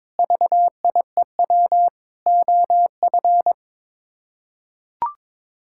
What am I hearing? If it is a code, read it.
Morse code, heard as a keyed tone, about 22 wpm, 700 Hz. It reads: VIEW OF
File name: practice / 22wpm / Build A Phrase - 5-Word Phrases - 22wpm